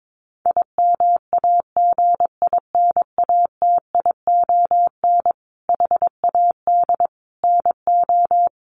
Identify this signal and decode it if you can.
Morse code, heard as a keyed tone, about 22 wpm, 700 Hz. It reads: IMAGINATION HAD NO
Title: Sentences from Top 1300 Words 22wpm